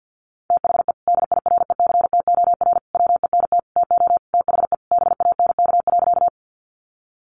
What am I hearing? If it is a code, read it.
Morse code, heard as a keyed tone, about 50 wpm, 700 Hz. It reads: THE DIRECTOR WENT TO THE BANK.
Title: Sentences from Top 1300 Words 50wpm